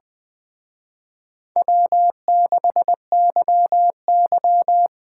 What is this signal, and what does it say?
Morse code, heard as a keyed tone, about 20 wpm, 700 Hz. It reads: W6YY
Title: Speed Racing - USA Call Signs 17wpm